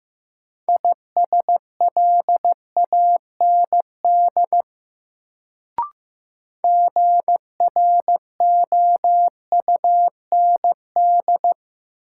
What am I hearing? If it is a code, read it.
Morse code, heard as a keyed tone, about 15 wpm, 700 Hz. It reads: ISLAND GROUND
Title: Sentences from Top 500 Words - New Word Drill - 15wpm